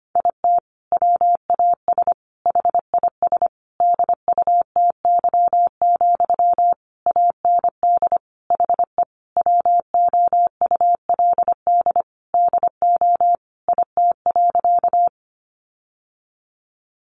Morse code, heard as a keyed tone, about 25 wpm, 700 Hz. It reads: IT WAS HIS DUTY, AND HE WOULD DO IT.
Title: Sentences from Top 1500 Words 25wpm